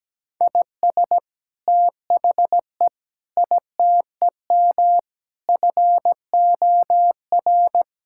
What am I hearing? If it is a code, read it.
Morse code, heard as a keyed tone, about 17 wpm, 700 Hz. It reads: IS THE ITEM FOR
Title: Sentences from Top 1000 Words - Review Part 6 - 17wpm